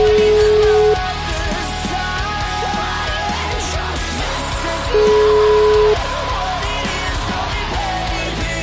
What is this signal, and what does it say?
Ringback tone (Europe, 425 Hz), with noise.